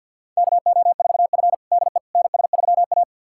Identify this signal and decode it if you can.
Morse code, heard as a keyed tone, about 50 wpm, 700 Hz. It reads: KO4F DE NS3A
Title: USA Callsign DE Callsign - 4 Characters - 50wpm